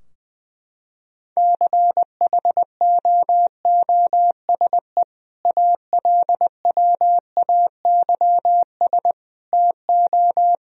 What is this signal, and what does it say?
Morse code, heard as a keyed tone, about 20 wpm, 700 Hz. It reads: CHOOSE ALWAYS TO
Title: Sentences from Top 1000 Words - Review Part 4 - 20wpm